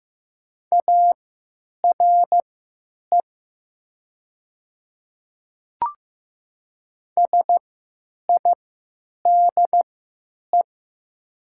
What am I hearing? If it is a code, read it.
Morse code, heard as a keyed tone, about 15 wpm, spaced as for 9 wpm, 700 Hz. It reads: ARE SIDE
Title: 3x Character Spacing - Top 100 Words - 15wpm